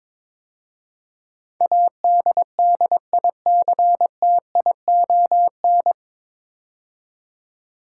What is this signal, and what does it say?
Morse code, heard as a keyed tone, about 22 wpm, 700 Hz. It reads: ADDICTION
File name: 9 Letter Words - First 3 Letters Given - 22wpm